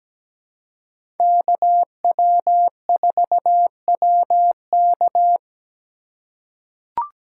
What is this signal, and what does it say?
Morse code, heard as a keyed tone, about 17 wpm, 700 Hz. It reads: KW4WK
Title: USA Call Signs - 4x Character Spacing - 17wpm